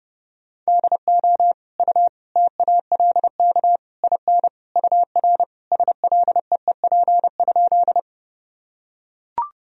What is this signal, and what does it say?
Morse code, heard as a keyed tone, about 30 wpm, 700 Hz. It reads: DO U TALK IN UR SLEEP?
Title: Sentences from Top 500 Words and Abbreviations 30wpm